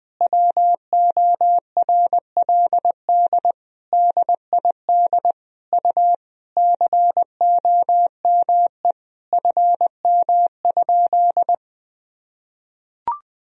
Morse code, heard as a keyed tone, about 20 wpm, 700 Hz. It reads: WORLD DID U COME FM?
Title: Sentences from Top 200 Words and Abbreviations 20wpm